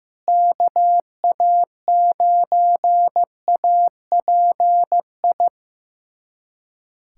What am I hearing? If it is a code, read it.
Morse code, heard as a keyed tone, about 15 wpm, 700 Hz. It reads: KA9API
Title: USA Callsign DE Callsign - 6 Characters - 15wpm